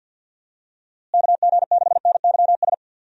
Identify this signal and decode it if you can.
Morse code, heard as a keyed tone, about 50 wpm, 700 Hz. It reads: KG6NYS
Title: USA Call Signs - 6 Characters - Repeated 3x - 50wpm